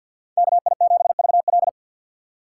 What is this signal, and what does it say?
Morse code, heard as a keyed tone, about 64 wpm, 700 Hz. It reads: KI7VL